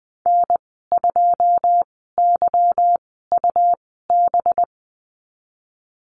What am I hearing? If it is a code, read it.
Morse code, heard as a keyed tone, about 20 wpm, spaced as for 15 wpm, 700 Hz. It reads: N2YUB
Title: 186 Lesson 30g - Letter b - Call Signs 20wpm